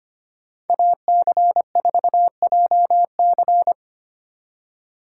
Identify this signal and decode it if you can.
Morse code, heard as a keyed tone, about 29 wpm, 700 Hz. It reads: AC4JC
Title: Speed Racing - USA Call Signs 25wpm